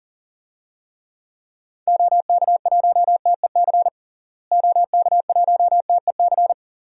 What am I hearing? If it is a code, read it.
Morse code, heard as a keyed tone, about 40 wpm, 700 Hz. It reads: OK1TEC OK1TEC
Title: DXCC-100 Call Signs - 6 Characters - Repeated 3x - 40wpm